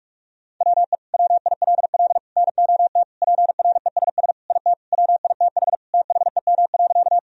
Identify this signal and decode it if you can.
Morse code, heard as a keyed tone, about 46 wpm, 700 Hz. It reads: WE WILL NOT PRESS IT WITH THEM.
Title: Sentences from Top 1000 Words - Review Part 1 - 45wpm